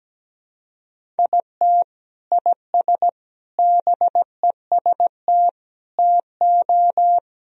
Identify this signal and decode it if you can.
Morse code, heard as a keyed tone, about 17 wpm, 700 Hz. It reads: IT IS BEST TO
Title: Sentences from Top 1000 Words - Review Part 1 - 17wpm